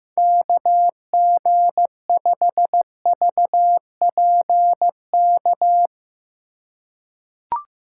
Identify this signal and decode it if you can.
Morse code, heard as a keyed tone, about 15 wpm, 700 Hz. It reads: KG5VPK
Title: USA Call Signs - 3x Character Spacing - 15wpm